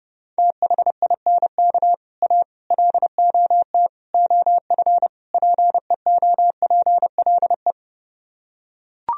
Morse code, heard as a keyed tone, about 30 wpm, 700 Hz. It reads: THINK A LOT OF PEOPLE